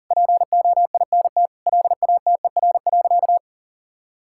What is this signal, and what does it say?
Morse code, heard as a keyed tone, about 40 wpm, 700 Hz. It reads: POINT LATER.
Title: Sentences from Top 1000 Words - Review Part 6 - 40wpm